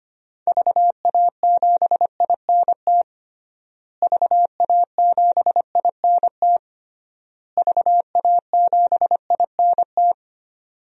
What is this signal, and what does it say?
Morse code, heard as a keyed tone, about 25 wpm, 700 Hz. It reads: VA7INT VA7INT VA7INT
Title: DXCC-100 Call Signs - 6 Characters - Repeated 3x - 25wpm